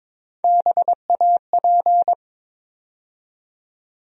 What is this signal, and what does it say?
Morse code, heard as a keyed tone, about 21 wpm, 700 Hz. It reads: BAP
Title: USA Call Sign Postfixes - 3 Letters 22wpm